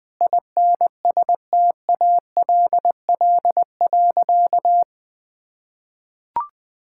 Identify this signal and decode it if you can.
Morse code, heard as a keyed tone, about 20 wpm, 700 Hz. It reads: INSTALL.
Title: Sentences from Top 1600 Words 20wpm